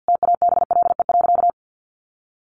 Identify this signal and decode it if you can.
Morse code, heard as a keyed tone, about 50 wpm, 700 Hz. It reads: TABLE.
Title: Sentences from Top 1500 Words 50wpm